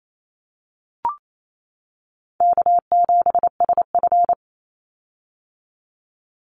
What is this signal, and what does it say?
Morse code, heard as a keyed tone, about 28 wpm, 700 Hz. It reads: K7SF
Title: Easy Call Signs - 28wpm